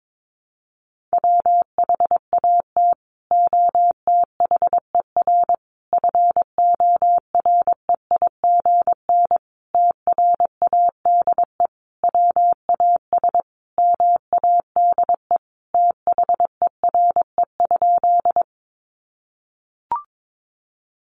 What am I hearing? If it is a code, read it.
Morse code, heard as a keyed tone, about 22 wpm, 700 Hz. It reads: WHAT OTHER FOREIGN TRADE WAS MADE THERE?
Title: Sentences from Top 1400 Words 22wpm